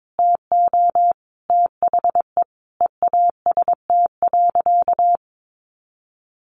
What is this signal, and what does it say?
Morse code, heard as a keyed tone, about 22 wpm, 700 Hz. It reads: TO THE EAST.
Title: Sentences from Top 1300 Words 22wpm